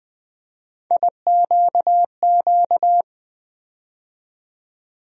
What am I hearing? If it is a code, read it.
Morse code, heard as a keyed tone, about 20 wpm, 700 Hz. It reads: IQQ
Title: USA Call Sign Postfixes - 3 Letters 20wpm